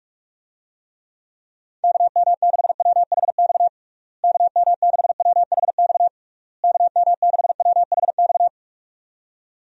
Morse code, heard as a keyed tone, about 46 wpm, 700 Hz. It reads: KM6WHX KM6WHX KM6WHX
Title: USA Call Signs - 6 Characters - Repeated 3x - 45wpm